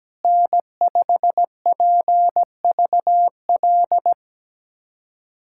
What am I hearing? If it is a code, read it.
Morse code, heard as a keyed tone, about 17 wpm, 700 Hz. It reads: N5PVL
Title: USA Callsign DE Callsign 17wpm